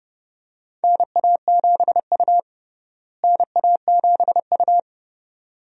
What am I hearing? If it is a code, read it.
Morse code, heard as a keyed tone, about 31 wpm, 700 Hz. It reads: NA7U NA7U
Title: USA Call Signs - 4 Characters - Repeated 2x - 30wpm